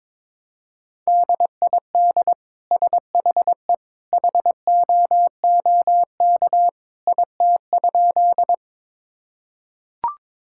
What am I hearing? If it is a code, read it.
Morse code, heard as a keyed tone, about 22 wpm, 700 Hz. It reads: DID SHE HOOK IT?
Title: Sentences from Top 1600 Words 22wpm